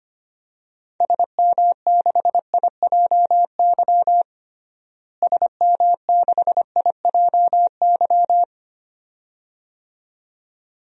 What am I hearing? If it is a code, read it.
Morse code, heard as a keyed tone, about 25 wpm, 700 Hz. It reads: SM6IJY SM6IJY
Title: DXCC-100 Call Signs - 6 Characters - Repeated 2x - 25wpm